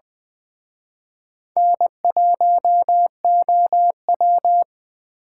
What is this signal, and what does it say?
Morse code, heard as a keyed tone, about 20 wpm, 700 Hz. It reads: N1OW
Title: Speed Racing - USA 1x2 Call Signs 17wpm